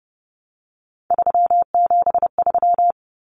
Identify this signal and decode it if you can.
Morse code, heard as a keyed tone, about 30 wpm, 700 Hz. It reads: 373
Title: Numbers - 3-Digit 30wpm